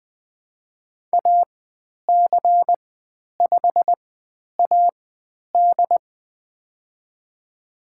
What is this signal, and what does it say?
Morse code, heard as a keyed tone, about 20 wpm, spaced as for 10 wpm, 700 Hz. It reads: AC5AD
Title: 109 Lesson 18g - Letter m - Call Signs 20wpm